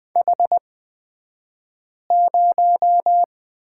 Morse code, heard as a keyed tone, about 20 wpm, spaced as for 5 wpm, 700 Hz. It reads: H0
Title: 254 Lesson 41d - Period - Two Characters 20wpm